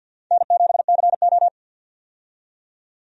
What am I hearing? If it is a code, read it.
Morse code, heard as a keyed tone, about 47 wpm, 700 Hz. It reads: N7CO
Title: USA Call Signs - 4 Characters - 50wpm